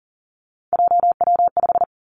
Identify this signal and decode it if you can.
Morse code, heard as a keyed tone, about 38 wpm, 700 Hz. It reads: JW5
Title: Three Challenging Letter-Number Combinations 40wpm